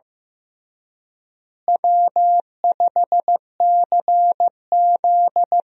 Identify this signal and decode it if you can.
Morse code, heard as a keyed tone, about 15 wpm, 700 Hz. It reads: W5CZ
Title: USA Call Signs - 4 Characters - Repeated 2x - 15wpm